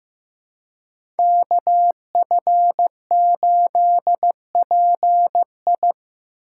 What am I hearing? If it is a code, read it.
Morse code, heard as a keyed tone, about 15 wpm, 700 Hz. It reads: KF8PI
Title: USA Call Signs - 5 Characters - 2x Character Spacing - 15wpm